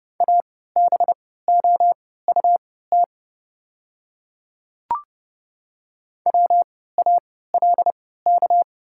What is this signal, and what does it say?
Morse code, heard as a keyed tone, about 30 wpm, spaced as for 17 wpm, 690 Hz. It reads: ABOUT WALK
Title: 3x Character Spacing - Top 300 Words - 30wpm